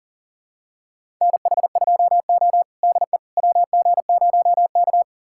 Morse code, heard as a keyed tone, about 40 wpm, 700 Hz. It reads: NH2O DE WG0K
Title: USA Callsign DE Callsign - 4 Characters - 40wpm